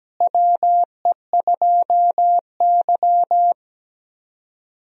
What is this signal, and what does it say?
Morse code, heard as a keyed tone, about 17 wpm, 700 Hz. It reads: WE2Y
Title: USA Call Signs - 4 Characters - Repeated 2x - 17wpm